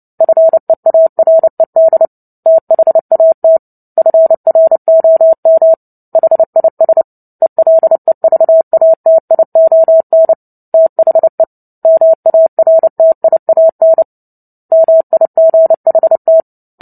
Morse code, heard as a keyed tone, about 30 wpm, 650 Hz. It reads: FEARED THAT FROM HIS ELEVATION THE MARTIAN MIGHT
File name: Book Two / 03-Chapter 3 - 29 WPM